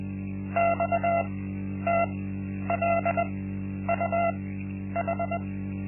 Morse code, heard as a keyed tone, about 21 wpm, spaced as for 10 wpm, 680 Hz. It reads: XTLUH
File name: Linksmakalnis-1